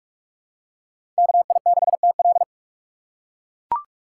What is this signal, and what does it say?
Morse code, heard as a keyed tone, about 44 wpm, 700 Hz. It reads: KI6TL
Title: USA Call Signs - 5 Characters - 2x Character Spacing - 45wpm